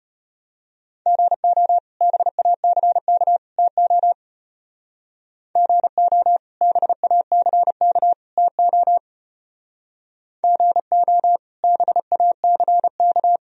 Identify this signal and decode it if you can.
Morse code, heard as a keyed tone, about 33 wpm, 700 Hz. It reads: GO BACK TO GO BACK TO GO BACK